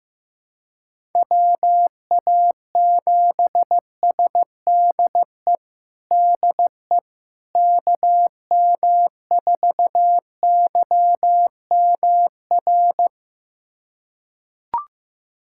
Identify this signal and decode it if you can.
Morse code, heard as a keyed tone, about 15 wpm, 700 Hz. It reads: WA7SDE DE KM4YMR
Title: USA Callsign DE Callsign - 6 Characters - 15wpm